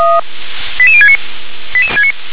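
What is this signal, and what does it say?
Five-tone selective call (ZVEI), heard as quick runs of five tones, tones of about 70 ms. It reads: 88069
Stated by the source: ZVEI five-tone selective call received by SDR.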